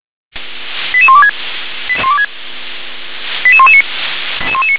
Five-tone selective call (ZVEI), heard as quick runs of five tones, tones of about 70 ms. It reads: 88126 88119